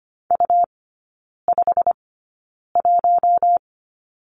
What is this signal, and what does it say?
Morse code, heard as a keyed tone, about 25 wpm, spaced as for 9 wpm, 700 Hz. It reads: U51
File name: Lesson 17e - Letter u - Three Characters 25wpm